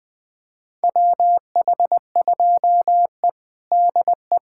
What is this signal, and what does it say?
Morse code, heard as a keyed tone, about 20 wpm, 700 Hz. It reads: WH2E DE
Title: USA Callsign DE Callsign - 4 Characters - 20wpm